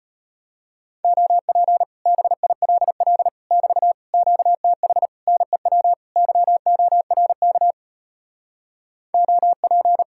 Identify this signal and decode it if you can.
Morse code, heard as a keyed tone, about 33 wpm, 700 Hz. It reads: OP BILL <BT> QTH NEW YORK OP